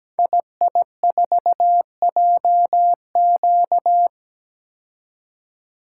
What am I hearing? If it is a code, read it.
Morse code, heard as a keyed tone, about 17 wpm, 700 Hz. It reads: II4JQ
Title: Code Groups - 5 Letter:Numbers - 17wpm